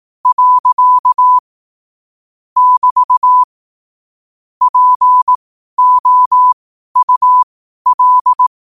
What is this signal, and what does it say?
Morse code, heard as a keyed tone, about 18 wpm, spaced as for 12 wpm, 1000 Hz. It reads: . <BT> POUL
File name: qotdfiles / 12wpm